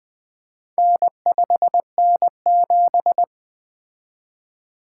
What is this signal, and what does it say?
Morse code, heard as a keyed tone, about 20 wpm, 700 Hz. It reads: N5N7
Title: Four Letter-Number Combinations - 20wpm